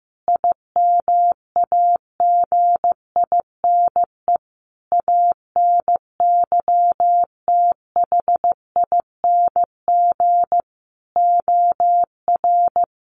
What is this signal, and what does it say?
Morse code, heard as a keyed tone, about 15 wpm, 700 Hz. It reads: IMAGINE ANYTHING OR
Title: Sentences from Top 1400 Words 15wpm